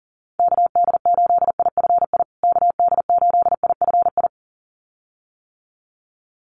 Morse code, heard as a keyed tone, about 40 wpm, 700 Hz. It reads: KD8IFI KD8IFI
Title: USA Call Signs Repeated 2x 40wpm